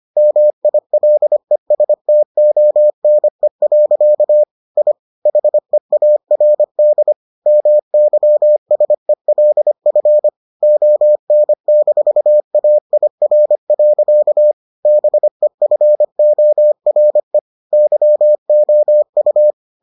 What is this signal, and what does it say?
Morse code, heard as a keyed tone, about 25 wpm, 600 Hz. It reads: MILESTONE. I HEARD MYSELF ON-AIR. BEFORE YOU